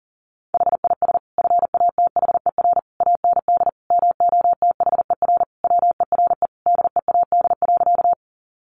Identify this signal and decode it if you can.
Morse code, heard as a keyed tone, about 40 wpm, 700 Hz. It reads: HIS FATHER AND MOTHER WERE DEAD.
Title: Sentences from Top 1500 Words 40wpm